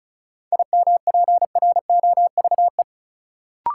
Morse code, heard as a keyed tone, about 36 wpm, 700 Hz. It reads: IMPROVE
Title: Sentences from Top 800 Words - New Word Drill - 35wpm